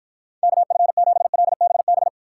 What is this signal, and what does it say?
Morse code, heard as a keyed tone, about 49 wpm, 700 Hz. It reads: KF7LBB